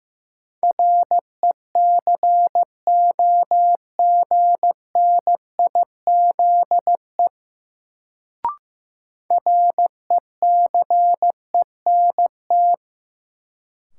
Morse code, heard as a keyed tone, about 15 wpm, 700 Hz. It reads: RECOGNIZE RECENT
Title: Sentences from Top 1400 Words - New Word Drill Alt - 15wpm